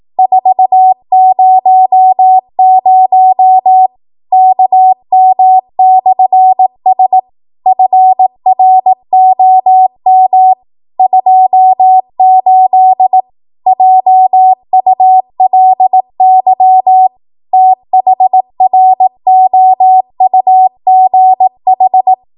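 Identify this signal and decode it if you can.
Morse code, heard as a keyed tone, about 18 wpm, 750 Hz. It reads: 400 KM/S FROM 28 JULY THROUGH
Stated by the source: ARRL 18 wpm Morse code practice archive transmission.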